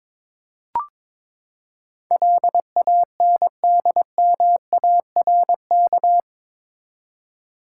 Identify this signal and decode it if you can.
Morse code, heard as a keyed tone, about 21 wpm, 700 Hz. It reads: LANDMARK
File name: practice / 22wpm / Eight Letter Words - 22wpm